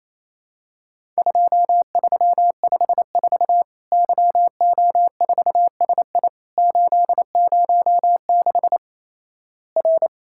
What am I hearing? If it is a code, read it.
Morse code, heard as a keyed tone, about 28 wpm, 700 Hz. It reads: 2354 YO4SI 806 R
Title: Worked All Europe DX Contest - QTC Exchange - 28wpm